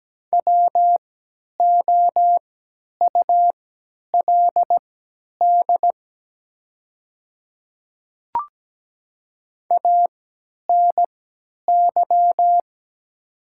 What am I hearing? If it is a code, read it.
Morse code, heard as a keyed tone, about 17 wpm, spaced as for 10 wpm, 700 Hz. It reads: WOULD ANY
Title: 3x Character Spacing - Top 200 Words - 17wpm